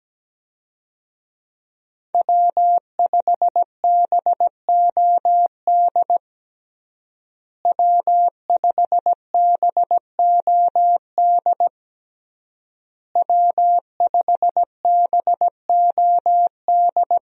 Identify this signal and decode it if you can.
Morse code, heard as a keyed tone, about 17 wpm, 700 Hz. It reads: W5BOD W5BOD W5BOD
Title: USA Call Signs - 5 Characters - Repeated 3x - 17wpm